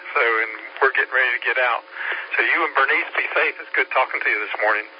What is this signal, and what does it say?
Emergency Alert System (SAME) tones, heard as harsh buzzing bursts, with speech. It is heard from 0.3 to 3.9 s.